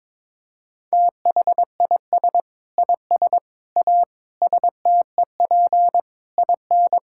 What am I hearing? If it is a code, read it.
Morse code, heard as a keyed tone, about 22 wpm, 700 Hz. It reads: THIS IS A STEP IN
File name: practice / 22wpm / Sets of 8 Words - 22wpm